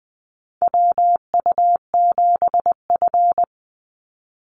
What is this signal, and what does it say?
Morse code, heard as a keyed tone, about 20 wpm, 700 Hz. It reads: WU7F
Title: Call Signs - 20wpm